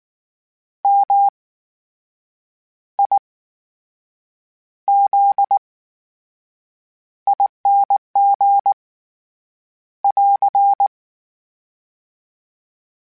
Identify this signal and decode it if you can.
Morse code, heard as a keyed tone, about 19 wpm, spaced as for 7 wpm, 800 Hz. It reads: MIZING <AR>
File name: Special INGs / Special INGs-mp3s / VerySlowToFaster Ing @18